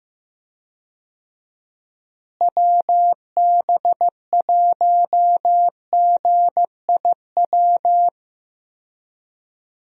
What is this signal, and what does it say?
Morse code, heard as a keyed tone, about 15 wpm, 700 Hz. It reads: WB1GIW
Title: USA Call Signs - 6 Characters - Repeated 3x - 15wpm